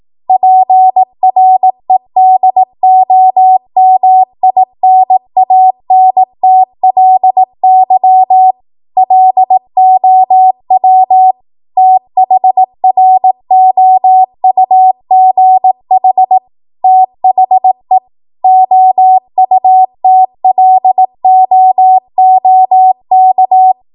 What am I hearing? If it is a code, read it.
Morse code, heard as a keyed tone, about 18 wpm, 750 Hz. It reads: PREDOMINANTLY LOW THROUGH THE OUTLOOK
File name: Morse / 18WPM